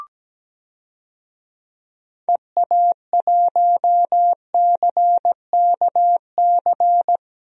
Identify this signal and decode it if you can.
Morse code, heard as a keyed tone, about 17 wpm, 700 Hz. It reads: EA1CKC
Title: DXCC-100 Call Signs - 6 Characters - 17wpm